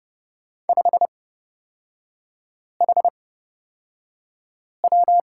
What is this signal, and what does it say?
Morse code, heard as a keyed tone, about 29 wpm, spaced as for 5 wpm, 700 Hz. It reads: 5HW
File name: Lesson 36e - BT - Three Characters 30wpm